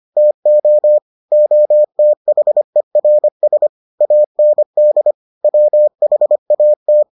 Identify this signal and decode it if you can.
Morse code, heard as a keyed tone, about 25 wpm, 600 Hz. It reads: TO OTHERS AND WHAT